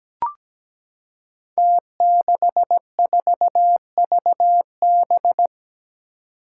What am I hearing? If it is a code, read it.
Morse code, heard as a keyed tone, about 17 wpm, 700 Hz. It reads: T64VB
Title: Five Challenging Letter-Number Combinations 17wpm